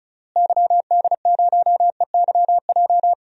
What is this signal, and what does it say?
Morse code, heard as a keyed tone, about 36 wpm, 700 Hz. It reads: YD0EYJ
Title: DXCC-100 Call Signs - 6 Characters - Repeated 2x - 35wpm